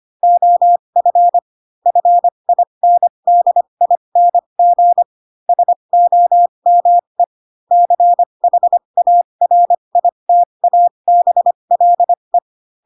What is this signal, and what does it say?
Morse code, heard as a keyed tone, about 25 wpm, spaced as for 22 wpm, 700 Hz. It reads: OF FINDING SOME CHARITABLE